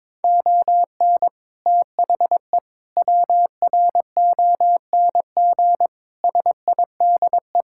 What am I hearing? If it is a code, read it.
Morse code, heard as a keyed tone, about 22 wpm, 700 Hz. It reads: ON THE WRONG SIDE
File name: practice / 22wpm / Sets of 6 Words - 22wpm